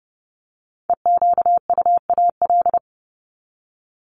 Morse code, heard as a keyed tone, about 30 wpm, 700 Hz. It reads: EQUAL
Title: LY Suffix - Top 100 Words with Base Words 30wpm